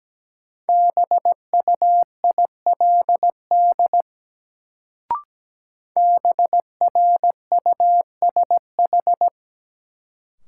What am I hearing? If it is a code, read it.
Morse code, heard as a keyed tone, about 17 wpm, 700 Hz. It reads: BUILD BRUSH
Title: Sentences from Top 1100 Words - New Word Drill Alt - 17wpm